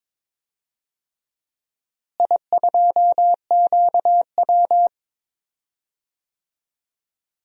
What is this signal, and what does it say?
Morse code, heard as a keyed tone, about 22 wpm, 700 Hz. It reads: I2QW
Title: DXCC-100 Call Signs - 4 Characters - Repeated 3x - 22wpm